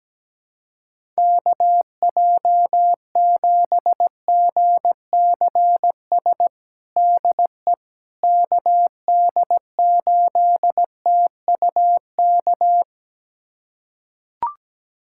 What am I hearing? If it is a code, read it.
Morse code, heard as a keyed tone, about 17 wpm, 700 Hz. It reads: KJ7GCS DE KD8TUK
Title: USA Callsign DE Callsign 17wpm